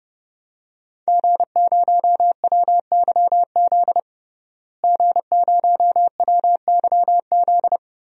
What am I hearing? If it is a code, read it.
Morse code, heard as a keyed tone, about 30 wpm, 700 Hz. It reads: G0WYZ G0WYZ
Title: DXCC-100 Call Signs - 5 Characters - Repeated 3x - 30wpm